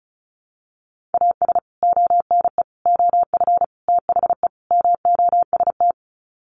Morse code, heard as a keyed tone, about 34 wpm, 700 Hz. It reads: AS ONE OF THE MOST
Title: Sets of 5 Words 35wpm